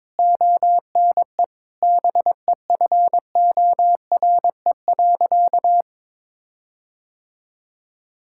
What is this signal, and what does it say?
Morse code, heard as a keyed tone, about 22 wpm, 700 Hz. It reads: ONE BEFORE.
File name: practice / 22wpm / Sentences with 8-Word Phrases - 22wpm